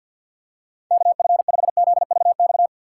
Morse code, heard as a keyed tone, about 51 wpm, 690 Hz. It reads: KF5ZVX
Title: USA Call Signs - 6x Character Spacing - 50wpm